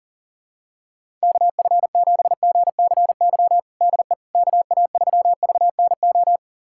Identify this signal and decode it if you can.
Morse code, heard as a keyed tone, about 40 wpm, 700 Hz. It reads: KF7GCY DE KA3VNO
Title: USA Callsign DE Callsign - 6 Characters - 40wpm